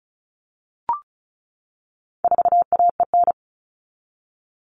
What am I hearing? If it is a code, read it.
Morse code, heard as a keyed tone, about 34 wpm, 700 Hz. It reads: 4AEN
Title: All Cut Numbers - 4-Digit 35wpm